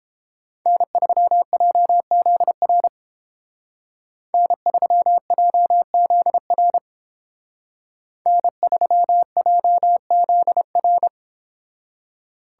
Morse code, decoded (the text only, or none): N3JZR N3JZR N3JZR